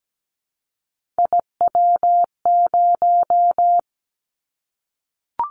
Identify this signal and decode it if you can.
Morse code, heard as a keyed tone, about 17 wpm, 700 Hz. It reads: IW0
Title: Callsign Prefixes for Top 25 DXCC - Repeated 3x - 17wpm